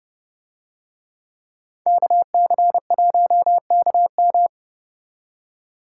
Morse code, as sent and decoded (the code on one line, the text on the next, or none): -.- -.-. .---- -.- --
KC1KM